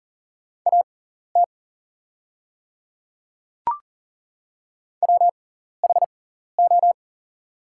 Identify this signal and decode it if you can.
Morse code, heard as a keyed tone, about 40 wpm, spaced as for 14 wpm, 700 Hz. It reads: AT WHO
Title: 6x Character Spacing - Top 100 Words - 40wpm